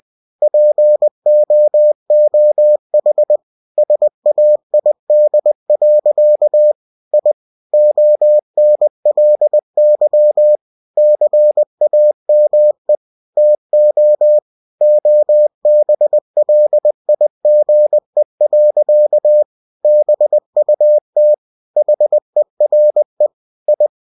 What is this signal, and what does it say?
Morse code, heard as a keyed tone, about 20 wpm, 600 Hz. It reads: POOH SAID. I ONLY CAME TO OBLIGE. BUT HERE I
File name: cw-books / winnie-the-pooh / winnie-20wpm-page-0079